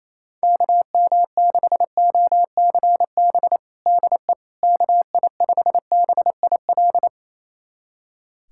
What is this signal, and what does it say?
Morse code, heard as a keyed tone, about 28 wpm, 700 Hz. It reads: KM6OCB DE KI5BIL
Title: USA Callsign DE Callsign - 6 Characters - 28wpm